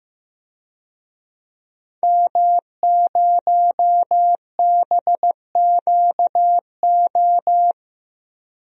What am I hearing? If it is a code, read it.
Morse code, heard as a keyed tone, about 15 wpm, 700 Hz. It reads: M0BQO
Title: DXCC-100 Call Signs - 5 Characters - Repeated 2x - 15wpm